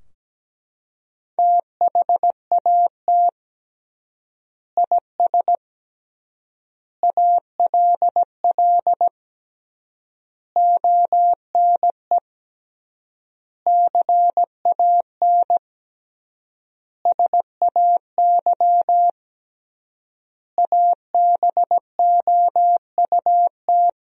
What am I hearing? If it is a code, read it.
Morse code, heard as a keyed tone, about 17 wpm, 700 Hz. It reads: THAT IS ALL ONE CAN SAY ABOUT
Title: 3x Word-Spacing - Sentences from Top 200 Words - 17wpm